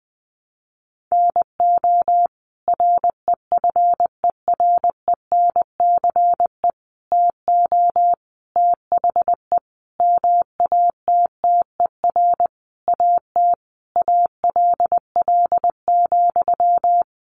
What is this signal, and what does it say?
Morse code, heard as a keyed tone, about 20 wpm, 700 Hz. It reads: NO REFERENCE TO THE MATTER AT ALL,
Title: Sentences from Top 1200 Words 20wpm